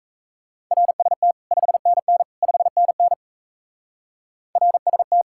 Morse code, heard as a keyed tone, about 37 wpm, 700 Hz. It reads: RST 5NN 5NN RST